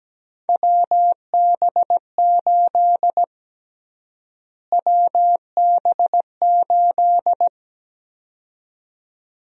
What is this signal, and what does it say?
Morse code, heard as a keyed tone, about 17 wpm, 700 Hz. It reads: WB8 WB8
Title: USA Call Sign Prefixes - Repeated 2x - 17wpm